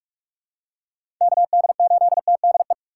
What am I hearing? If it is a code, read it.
Morse code, heard as a keyed tone, about 43 wpm, 700 Hz. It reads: KD8TDE
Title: USA Call Signs - 4x Character Spacing - 45wpm